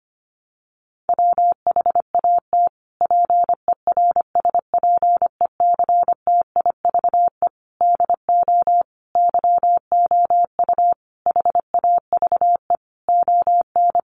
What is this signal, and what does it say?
Morse code, heard as a keyed tone, about 25 wpm, 700 Hz. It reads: WHAT PERSPECTIVE DO YOU HAVE ON
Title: Sentences from Top 1200 Words 25wpm